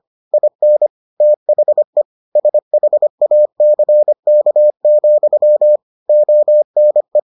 Morse code, heard as a keyed tone, about 25 wpm, 600 Hz. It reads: IN THE SHACK, ONE